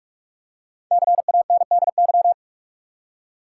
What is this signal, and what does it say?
Morse code, heard as a keyed tone, about 46 wpm, 700 Hz. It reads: CANDY
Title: Five Letter Words - 45wpm